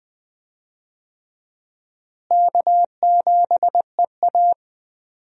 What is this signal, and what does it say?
Morse code, heard as a keyed tone, about 20 wpm, 700 Hz. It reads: K7EA
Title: USA Call Signs - 4 Characters - Repeated 3x - 20wpm